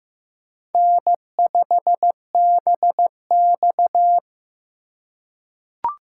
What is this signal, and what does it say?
Morse code, heard as a keyed tone, about 15 wpm, 700 Hz. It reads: N5BX
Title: USA Call Signs - 4 Characters - 2x Character Spacing - 15wpm